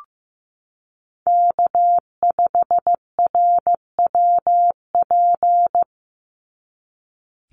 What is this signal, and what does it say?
Morse code, heard as a keyed tone, about 15 wpm, 700 Hz. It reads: K5RWP
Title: NA SOTA Chaser Callsigns 15wpm